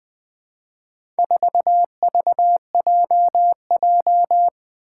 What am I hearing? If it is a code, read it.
Morse code, heard as a keyed tone, about 20 wpm, 700 Hz. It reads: 4VJJ
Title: Four Challenging Letter-Number Combinations 20wpm